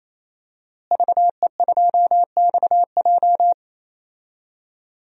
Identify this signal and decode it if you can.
Morse code, heard as a keyed tone, about 28 wpm, 700 Hz. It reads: VE2XJ
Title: DXCC-100 Call Signs - 5 Characters - Repeated 3x - 28wpm